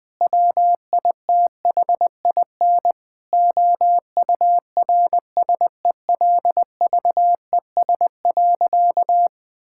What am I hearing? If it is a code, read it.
Morse code, heard as a keyed tone, about 20 wpm, 700 Hz. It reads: WITHIN OURSELVES.
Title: Sentences from Top 1600 Words 20wpm